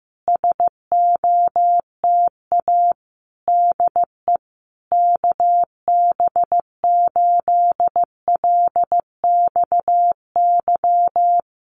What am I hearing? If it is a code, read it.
Morse code, heard as a keyed tone, about 15 wpm, 700 Hz. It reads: SOTA DE KB8LXY
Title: Common QSO Phrases Encore 15wpm